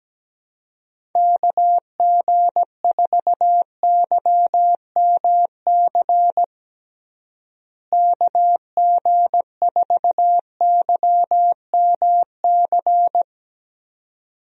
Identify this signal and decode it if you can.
Morse code, heard as a keyed tone, about 17 wpm, 700 Hz. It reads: KG4YMC KG4YMC
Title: USA Call Signs - 6 Characters - Repeated 2x - 17wpm